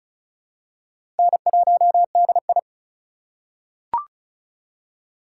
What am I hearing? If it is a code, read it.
Morse code, heard as a keyed tone, about 34 wpm, 700 Hz. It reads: N1DI
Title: USA Call Signs - 4x Character Spacing - 35wpm